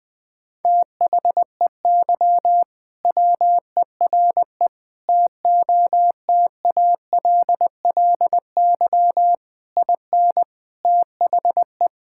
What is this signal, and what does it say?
Morse code, heard as a keyed tone, about 20 wpm, 700 Hz. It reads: THEY WERE TOTALLY IN THE
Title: Sentences from Top 1600 Words 20wpm